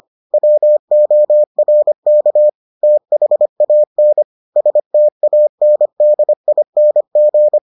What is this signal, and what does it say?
Morse code, heard as a keyed tone, about 25 wpm, 600 Hz. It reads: WORK THAN STANDING